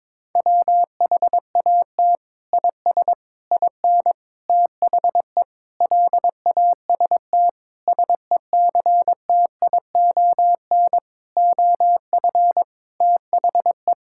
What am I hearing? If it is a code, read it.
Morse code, heard as a keyed tone, about 22 wpm, 700 Hz. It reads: WHAT IS IN THE LAST SECTION OF THE
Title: Sentences from Top 1000 Words - Review Part 5 - 22wpm